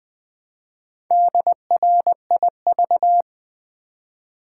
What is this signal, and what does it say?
Morse code, heard as a keyed tone, about 20 wpm, 700 Hz. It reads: DRIV
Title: Build A Word - 2 Letters At A Time - 6-Letter Words - 20wpm